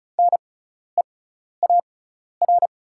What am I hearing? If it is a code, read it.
Morse code, heard as a keyed tone, about 34 wpm, spaced as for 12 wpm, 700 Hz. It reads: NEAR
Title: 6x Character Spacing - Top 300 Words - 35wpm